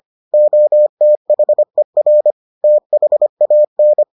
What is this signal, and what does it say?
Morse code, heard as a keyed tone, about 25 wpm, 600 Hz. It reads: OTHER THAN